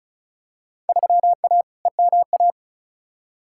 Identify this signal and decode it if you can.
Morse code, heard as a keyed tone, about 36 wpm, 700 Hz. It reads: 3A EMA
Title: Field Day Exchanges - 35wpm